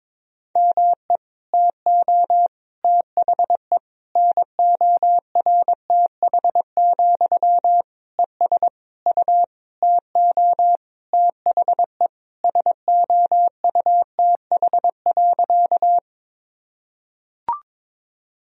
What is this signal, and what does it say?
Morse code, heard as a keyed tone, about 21 wpm, 700 Hz. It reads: ME TO THE NORTH, ES U TO THE SOUTH.
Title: Sentences from Top 400 Words and Abbreviations 22wpm